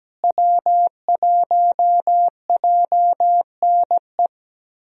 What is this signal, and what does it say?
Morse code, heard as a keyed tone, about 17 wpm, 700 Hz. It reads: W1JNE
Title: USA Call Signs - 5 Characters - Repeated 3x - 17wpm